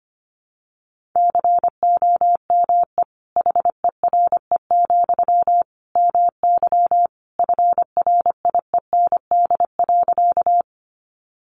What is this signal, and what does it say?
Morse code, heard as a keyed tone, about 25 wpm, 700 Hz. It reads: COME HERE, MY FRIEND.
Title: Sentences from Top 300 Words 25wpm